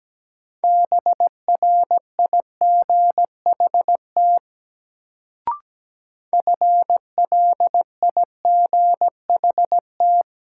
Morse code, heard as a keyed tone, about 17 wpm, 700 Hz. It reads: BRIGHT FLIGHT
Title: Sentences from Top 1000 Words - New Word Drill - 17wpm